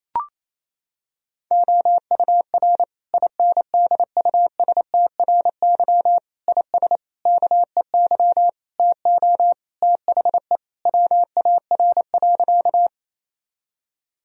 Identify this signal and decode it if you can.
Morse code, heard as a keyed tone, about 28 wpm, 700 Hz. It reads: OUR INDUSTRY IS KEY TO THE WAR.
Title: Sentences from Top 1000 Words - Review Part 4 - 28wpm